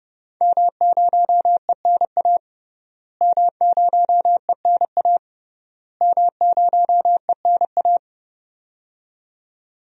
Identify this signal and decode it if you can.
Morse code, heard as a keyed tone, about 30 wpm, 700 Hz. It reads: M0ENA M0ENA M0ENA
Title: DXCC-100 Call Signs - 5 Characters - Repeated 3x - 30wpm